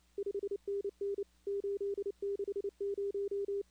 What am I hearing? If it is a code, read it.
Morse code, heard as a keyed tone, about 28 wpm, 400 Hz. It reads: 5NN 860
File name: contests / 2019-cw-wpx-kt5j-k5pi- -2330 0-13